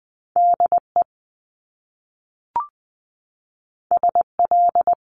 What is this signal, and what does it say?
Morse code, heard as a keyed tone, about 20 wpm, 700 Hz. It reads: DE SL
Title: Top Two Letter Combinations in Words 20wpm